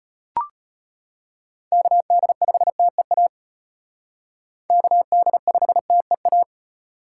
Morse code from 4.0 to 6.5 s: KD5TEA

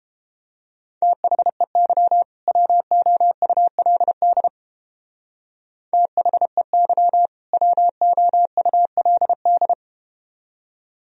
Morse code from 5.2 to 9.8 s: THEY WOULD

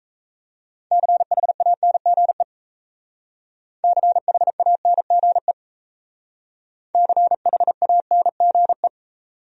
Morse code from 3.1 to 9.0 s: CHANGE CHANGE

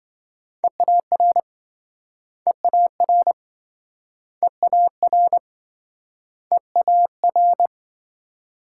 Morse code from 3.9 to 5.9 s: EAR

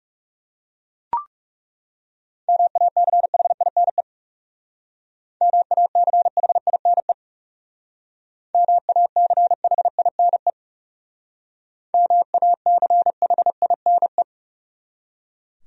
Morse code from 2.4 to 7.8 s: MACHINE MACHINE

